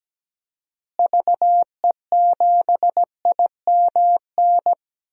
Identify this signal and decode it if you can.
Morse code, heard as a keyed tone, about 17 wpm, 700 Hz. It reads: VE7IMN
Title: DXCC-100 Call Signs - 6 Characters - Repeated 2x - 17wpm